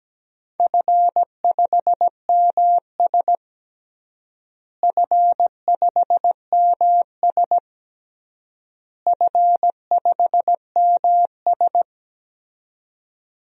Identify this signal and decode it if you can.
Morse code, heard as a keyed tone, about 17 wpm, 700 Hz. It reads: F5MS F5MS F5MS
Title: DXCC-100 Call Signs - 4 Characters - Repeated 3x - 17wpm